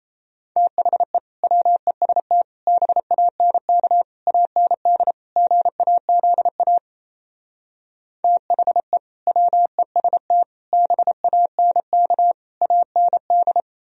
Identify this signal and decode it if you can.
Morse code, heard as a keyed tone, about 36 wpm, 700 Hz. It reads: THE WEST BANK AND GAZA THE WEST BANK AND